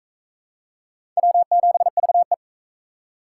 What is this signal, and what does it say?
Morse code, heard as a keyed tone, about 40 wpm, 700 Hz. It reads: W7VE